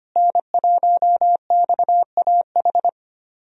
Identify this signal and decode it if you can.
Morse code, heard as a keyed tone, about 25 wpm, 700 Hz. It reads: N1XAH
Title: USA Call Signs - 5 Characters - Repeated 2x - 25wpm